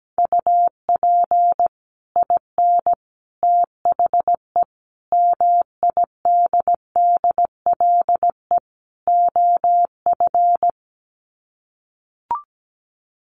Morse code, heard as a keyed tone, about 17 wpm, 700 Hz. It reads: UP IN THE MIDDLE OF
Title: Sets of 5 Words Encore 17wpm